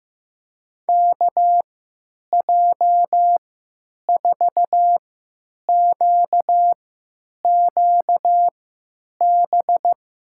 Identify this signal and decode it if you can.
Morse code, heard as a keyed tone, about 15 wpm, spaced as for 9 wpm, 700 Hz. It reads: KJ4QQB
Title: USA Call Signs - 6 Characters - 3x Character Spacing - 15wpm